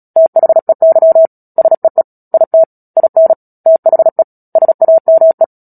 Morse code, heard as a keyed tone, about 37 wpm, 650 Hz. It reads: THEY SEE IT IN THE SAME
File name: Book Two / 10-Chapter 10 - Epilogue - 36 WPM